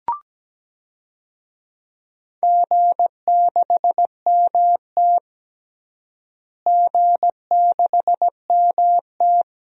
Morse code, heard as a keyed tone, about 17 wpm, 700 Hz. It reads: G6MT G6MT
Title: DXCC-100 Call Signs - 4 Characters - Repeated 3x - 17wpm